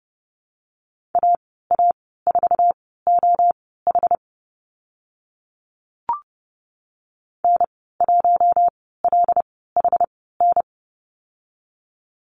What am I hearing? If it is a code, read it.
Morse code, heard as a keyed tone, about 30 wpm, spaced as for 17 wpm, 700 Hz. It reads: AA4OH N1LHN
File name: Lesson 14g - Number 2 - Call Signs 30wpm